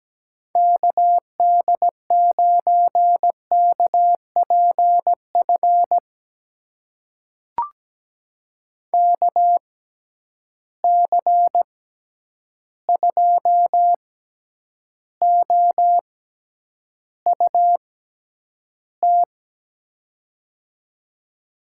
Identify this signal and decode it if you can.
Morse code, heard as a keyed tone, about 17 wpm, spaced as for 9 wpm, 700 Hz. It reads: KD9KPF KC2OUT
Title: USA Call Signs - 6 Characters - 6x Character Spacing - 17wpm